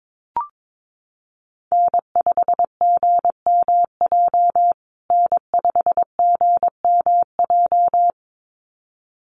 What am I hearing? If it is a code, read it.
Morse code, heard as a keyed tone, about 22 wpm, 700 Hz. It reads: N5GMJ N5GMJ
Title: USA Call Signs Repeated 2x 22wpm